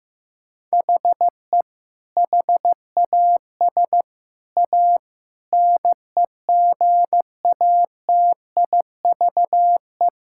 Morse code, heard as a keyed tone, about 15 wpm, 700 Hz. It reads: HE HAS A NEGATIVE
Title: Sentences from Top 1000 Words - Review Part 6 - 15wpm